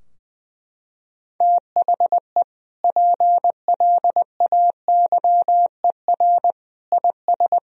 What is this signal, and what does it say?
Morse code, heard as a keyed tone, about 21 wpm, 700 Hz. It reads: THE PLAYER IS
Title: Sentences from Top 1000 Words - Review Part 4 - 20wpm